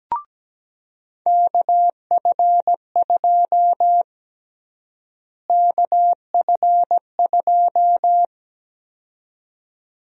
Morse code, heard as a keyed tone, about 17 wpm, 700 Hz. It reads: KF2 KF2
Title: USA Call Sign Prefixes - Repeated 2x - 17wpm